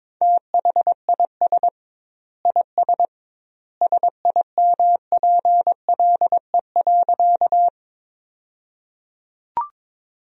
Morse code, heard as a keyed tone, about 22 wpm, 700 Hz. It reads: THIS IS SIMPLE.
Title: 2x Word-Spacing - Sentences from Top 400 Words - 22wpm